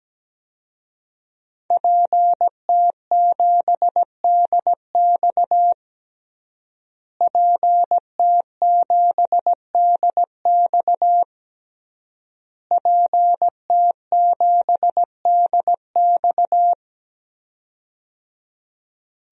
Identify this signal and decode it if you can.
Morse code, heard as a keyed tone, about 17 wpm, 700 Hz. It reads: PT7DX PT7DX PT7DX
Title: DXCC-100 Call Signs - 5 Characters - Repeated 3x - 17wpm